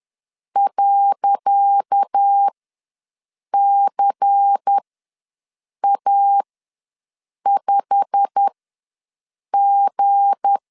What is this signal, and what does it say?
Morse code, heard as a keyed tone, about 10 wpm, spaced as for 6 wpm, 800 Hz. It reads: .CA5G
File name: LEZIONI / 10 SIGNS / 10C DOT ws 10wpm 4213